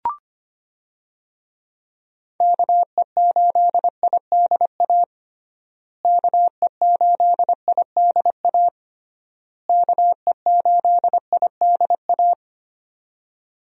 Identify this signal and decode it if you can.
Morse code, heard as a keyed tone, about 25 wpm, 700 Hz. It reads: KE8IDA KE8IDA KE8IDA
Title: USA Call Signs - 6 Characters - Repeated 3x - 25wpm